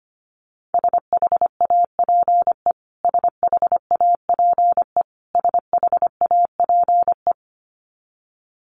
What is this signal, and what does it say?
Morse code, heard as a keyed tone, about 25 wpm, 700 Hz. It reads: SHAPE SHAPE SHAPE
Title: Top 500 Words Repeated 25wpm